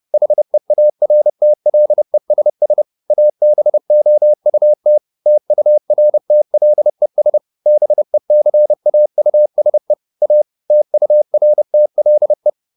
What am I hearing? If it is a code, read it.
Morse code, heard as a keyed tone, about 30 wpm, 600 Hz. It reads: HEARTLESS ABOUT TURTLES BECAUSE A TURTLE
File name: the old man and the sea / the old man and the sea 1 30wpm